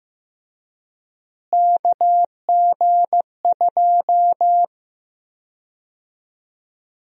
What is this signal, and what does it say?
Morse code, heard as a keyed tone, about 15 wpm, 700 Hz. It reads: KG2
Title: USA Call Sign Prefixes - Repeated 2x - 15wpm